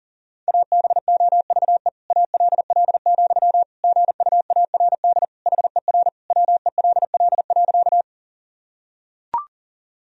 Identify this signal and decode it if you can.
Morse code, heard as a keyed tone, about 40 wpm, 700 Hz. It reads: ABOVE ALL, GUARD HER WELL.
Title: Sentences from Top 1000 Words - Review Part 4 - 40wpm